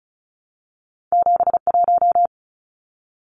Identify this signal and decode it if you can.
Morse code, heard as a keyed tone, about 36 wpm, 700 Hz. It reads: 71
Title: Numbers - 2-Digit 35wpm